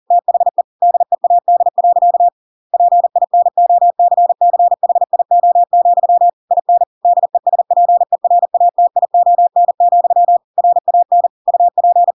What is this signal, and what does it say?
Morse code, heard as a keyed tone, about 40 wpm, 700 Hz. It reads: THE DEAD. PINOCCHIO, IN DESPERATION, RAN UP